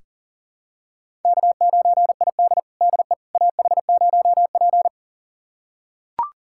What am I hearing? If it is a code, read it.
Morse code, heard as a keyed tone, about 40 wpm, 700 Hz. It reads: K9ID DE AH0P
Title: USA Callsign DE Callsign - 4 Characters - 40wpm